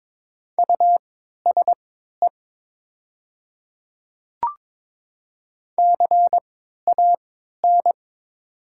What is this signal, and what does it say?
Morse code, heard as a keyed tone, about 22 wpm, spaced as for 12 wpm, 700 Hz. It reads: USE CAN
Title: 3x Character Spacing - Top 500 Words - 22wpm